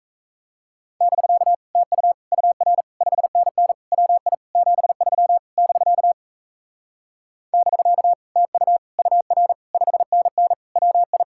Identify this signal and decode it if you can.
Morse code, heard as a keyed tone, about 37 wpm, 700 Hz. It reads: <BK> TU UR 5NN WI 73 <BK> <BK> TU UR 5NN WI